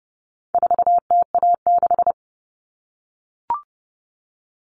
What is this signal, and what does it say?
Morse code, heard as a keyed tone, about 30 wpm, 700 Hz. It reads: 4TA6
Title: All Cut Numbers - 4-Digit 30wpm